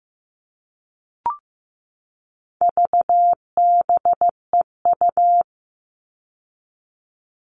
Morse code, heard as a keyed tone, about 15 wpm, 700 Hz. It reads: VBEU
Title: All Cut Numbers - 4-Digit 15wpm